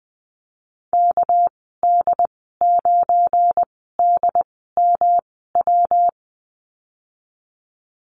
Morse code, heard as a keyed tone, about 20 wpm, spaced as for 15 wpm, 700 Hz. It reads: KD9DMW
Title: 219 Lesson 35g - Number 0 - Call Signs 20wpm